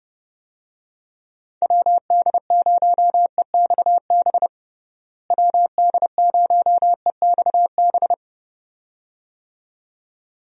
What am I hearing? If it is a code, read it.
Morse code, heard as a keyed tone, about 30 wpm, 700 Hz. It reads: WD0EXB WD0EXB
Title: USA Call Signs - 6 Characters - Repeated 2x - 30wpm